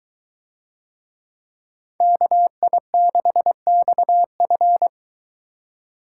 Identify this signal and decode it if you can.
Morse code, heard as a keyed tone, about 23 wpm, 700 Hz. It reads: KI6XF